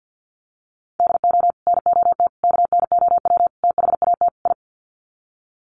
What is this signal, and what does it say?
Morse code, heard as a keyed tone, about 50 wpm, 700 Hz. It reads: DO NOT KNOW THAT I
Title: Sets of 5 Words Encore2 50wpm